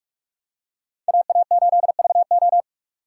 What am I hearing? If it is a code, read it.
Morse code, heard as a keyed tone, about 45 wpm, 700 Hz. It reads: AA8VO
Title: USA Call Signs - 5 Characters - 3x Character Spacing - 45wpm